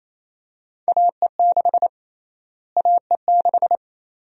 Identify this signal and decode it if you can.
Morse code, heard as a keyed tone, about 28 wpm, 700 Hz. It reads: AE6 AE6
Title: USA Call Sign Prefixes - Repeated 2x - 28wpm